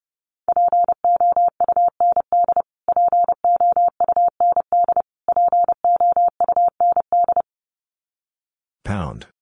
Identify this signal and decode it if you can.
Morse code, heard as a keyed tone, about 30 wpm, 700 Hz. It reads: POUND POUND POUND
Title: Top 500 Words Repeated 30wpm